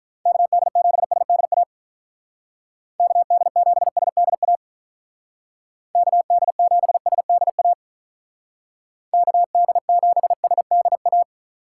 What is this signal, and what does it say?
Morse code, heard as a keyed tone, about 42 wpm, 700 Hz. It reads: KD7SDA KD7SDA KD7SDA KD7SDA
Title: Speed Racing - USA 2x3 Call Signs 35wpm